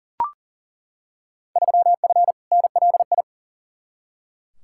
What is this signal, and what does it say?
Morse code, heard as a keyed tone, about 38 wpm, 700 Hz. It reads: 3F NLI
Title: Field Day Exchanges - 40wpm